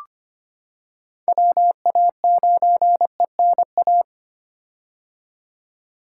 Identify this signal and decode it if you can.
Morse code, heard as a keyed tone, about 25 wpm, 700 Hz. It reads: WA9ENA
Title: K1USN SST QSO Practice - 25wpm